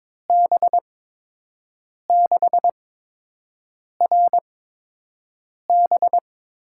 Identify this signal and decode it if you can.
Morse code, heard as a keyed tone, about 22 wpm, spaced as for 6 wpm, 700 Hz. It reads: B6RB
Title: USA Call Signs - 5 Characters - 8x Character Spacing - 22wpm